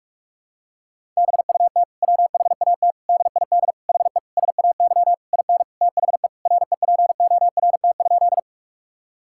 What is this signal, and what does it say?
Morse code, heard as a keyed tone, about 41 wpm, 700 Hz. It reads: BUT WHAT DID HE SAY IN THE REPORT?
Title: Sentences from Top 1000 Words - Review Part 6 - 45wpm